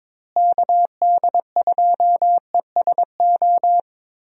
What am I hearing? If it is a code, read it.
Morse code, heard as a keyed tone, about 22 wpm, 700 Hz. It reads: KD2ESO
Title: USA Call Signs - 6 Characters - Repeated 3x - 22wpm